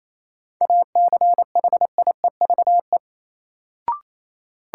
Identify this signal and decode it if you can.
Morse code, heard as a keyed tone, about 28 wpm, 700 Hz. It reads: ACHIEVE
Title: Sentences from Top 1000 Words - Review Part 5 - New Word Drill - 28wpm